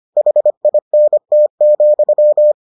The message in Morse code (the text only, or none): HINT,